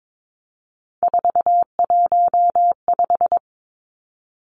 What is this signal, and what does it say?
Morse code, heard as a keyed tone, about 22 wpm, 700 Hz. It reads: 415
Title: Numbers - 3-Digit 22wpm